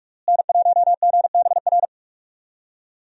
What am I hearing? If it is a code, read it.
Morse code, heard as a keyed tone, about 46 wpm, 700 Hz. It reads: N1GBR
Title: USA Call Signs - 5 Characters - Repeated 3x - 45wpm